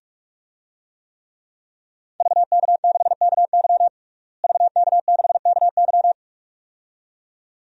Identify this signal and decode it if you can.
Morse code, heard as a keyed tone, about 45 wpm, 700 Hz. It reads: VK6KY VK6KY
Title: DXCC-100 Call Signs - 5 Characters - Repeated 2x - 45wpm